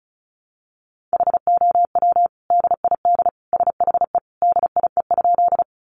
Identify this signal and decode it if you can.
Morse code, heard as a keyed tone, about 34 wpm, 700 Hz. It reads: HOW DID SHE DIE?
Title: Sentences from Top 1300 Words 35wpm